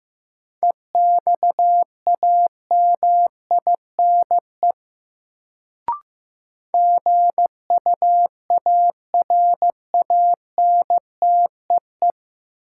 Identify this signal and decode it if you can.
Morse code, heard as a keyed tone, about 15 wpm, 700 Hz. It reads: EXAMINE GUARANTEE
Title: Sentences from Top 1600 Words - New Word Drill - 15wpm